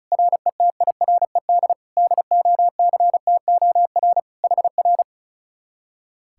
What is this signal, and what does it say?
Morse code, heard as a keyed tone, about 44 wpm, 700 Hz. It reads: RETIRED DOCTOR HR